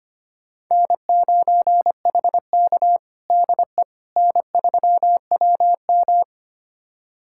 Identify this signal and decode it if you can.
Morse code, heard as a keyed tone, about 25 wpm, 700 Hz. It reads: N9HK DE N3WM
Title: USA Callsign DE Callsign - 4 Characters - 25wpm